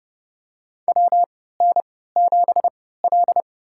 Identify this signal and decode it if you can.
Morse code, heard as a keyed tone, about 30 wpm, spaced as for 17 wpm, 700 Hz. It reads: WN7L
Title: USA Call Signs - 4 Characters - 3x Character Spacing - 30wpm